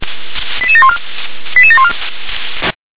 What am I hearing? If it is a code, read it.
Five-tone selective call (ZVEI), heard as quick runs of five tones, tones of about 70 ms. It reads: 88714 88714